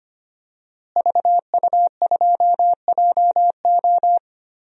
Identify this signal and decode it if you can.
Morse code, heard as a keyed tone, about 25 wpm, 700 Hz. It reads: VU2JO
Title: DXCC-100 Call Signs - 5 Characters - Repeated 3x - 25wpm